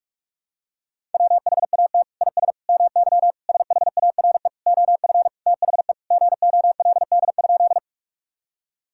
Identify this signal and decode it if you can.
Morse code, heard as a keyed tone, about 41 wpm, 700 Hz. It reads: WHAT IS MY SHARE OF THE GOLD?
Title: Sentences from Top 1000 Words - Review Part 6 - 45wpm